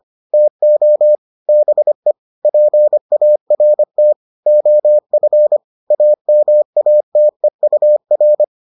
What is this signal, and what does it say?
Morse code, heard as a keyed tone, about 25 wpm, 600 Hz. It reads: TO BE PART OF AMATEUR